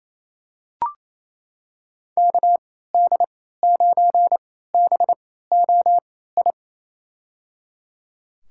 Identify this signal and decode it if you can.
Morse code, heard as a keyed tone, about 28 wpm, spaced as for 16 wpm, 690 Hz. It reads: KD9BOI
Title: USA Call Signs - 3x Character Spacing - 28wpm